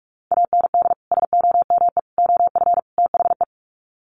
Morse code, heard as a keyed tone, about 44 wpm, 700 Hz. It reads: AND SOME OF THE
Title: Sets of 4 Words 45wpm